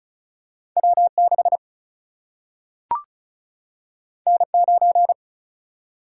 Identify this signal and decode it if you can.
Morse code, heard as a keyed tone, about 36 wpm, 700 Hz. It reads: W 6 N 9
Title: Single-Letter Single-Number 35wpm